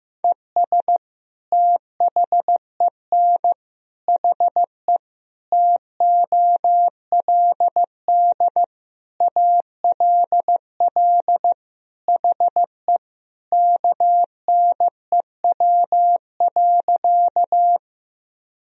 Morse code, heard as a keyed tone, about 15 wpm, 700 Hz. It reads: ES THEN HE TOLD ALL HE KNEW.
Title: Sentences from Top 400 Words and Abbreviations 15wpm